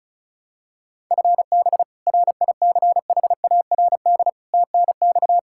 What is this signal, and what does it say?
Morse code, heard as a keyed tone, about 34 wpm, 700 Hz. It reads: FB RICHARD TNX
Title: Practice QSO - Exchange 3 - More Info - 35wpm